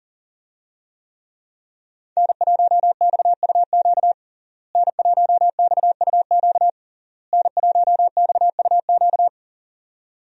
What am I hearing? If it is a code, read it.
Morse code, heard as a keyed tone, about 40 wpm, 700 Hz. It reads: N1XUQ N1XUQ N1XUQ
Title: USA Call Signs - 5 Characters - Repeated 3x - 40wpm